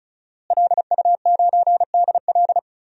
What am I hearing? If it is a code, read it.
Morse code, heard as a keyed tone, about 36 wpm, 700 Hz. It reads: LU9DL
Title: DXCC-100 Call Signs - 5 Characters - Repeated 2x - 35wpm